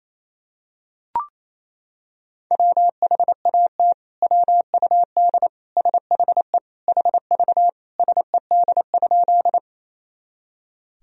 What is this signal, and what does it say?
Morse code, heard as a keyed tone, about 28 wpm, 700 Hz. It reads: WHAT WUD SHE HV SED?
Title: Sentences from Top 100 Words and Abbreviations 28wpm